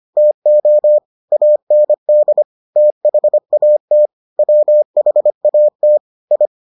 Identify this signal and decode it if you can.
Morse code, heard as a keyed tone, about 25 wpm, 600 Hz. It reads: TO AND THAT WHAT I